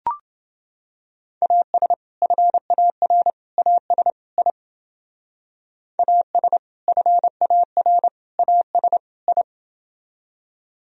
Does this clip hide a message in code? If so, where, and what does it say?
Yes: Morse code from 1.4 to 9.4 s, heard as a keyed tone, about 29 wpm, 700 Hz. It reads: AS FAR AS I AS FAR AS I